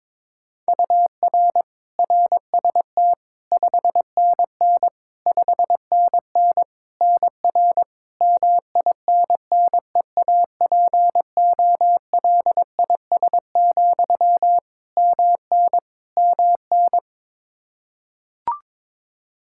Morse code, heard as a keyed tone, about 22 wpm, 700 Hz. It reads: UR RST 5NN 5NN NR MINNEAPOLIS, MN MN
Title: Practice QSO - Exchange 1 - Beginning - 22wpm